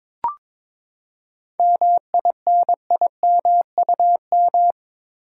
Morse code, heard as a keyed tone, about 22 wpm, 700 Hz. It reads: MINIMUM
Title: Seven Letter Words - 22wpm